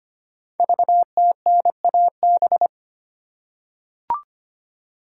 Morse code, heard as a keyed tone, about 25 wpm, 700 Hz. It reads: VTNAB
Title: Code Groups - 5 Letters - 25wpm